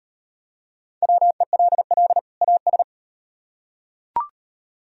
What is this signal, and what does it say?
Morse code, heard as a keyed tone, about 38 wpm, 700 Hz. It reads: WELL AS